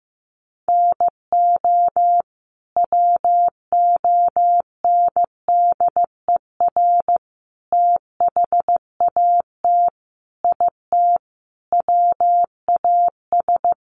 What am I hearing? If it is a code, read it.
Morse code, heard as a keyed tone, about 15 wpm, 700 Hz. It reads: NO WONDER THAT IT WAS
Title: Sentences from Top 1100 Words 15wpm